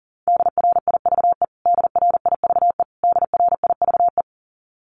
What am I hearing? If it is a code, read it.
Morse code, heard as a keyed tone, about 40 wpm, 700 Hz. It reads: DRIVE DRIVE DRIVE
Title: Top 500 Words Repeated 40wpm